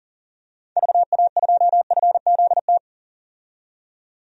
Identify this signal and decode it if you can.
Morse code, heard as a keyed tone, about 40 wpm, 700 Hz. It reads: VA2FZT
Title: DXCC-100 Call Signs - 6 Characters - 40wpm